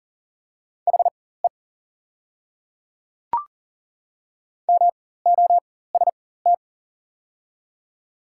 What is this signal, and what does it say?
Morse code, heard as a keyed tone, about 40 wpm, spaced as for 19 wpm, 700 Hz. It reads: HE MOST
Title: 4x Character Spacing - Top 100 Words - 40wpm